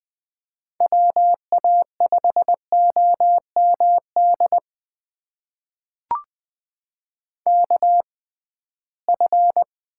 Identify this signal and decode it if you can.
Morse code, heard as a keyed tone, about 20 wpm, spaced as for 15 wpm, 700 Hz. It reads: WA5OMD KF
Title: USA Call Signs - 6x Character Spacing - 20wpm